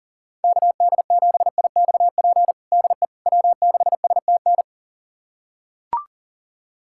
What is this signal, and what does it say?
Morse code, heard as a keyed tone, about 40 wpm, 700 Hz. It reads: KD7IXP DE W6STN
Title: USA Callsign DE Callsign 40wpm